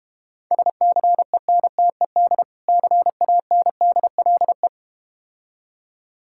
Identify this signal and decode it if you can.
Morse code, heard as a keyed tone, about 31 wpm, 690 Hz. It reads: SCENTED CANDLE
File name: practice / 25wpm / Difficult Words - Speed-Racing with Hints - 25wpm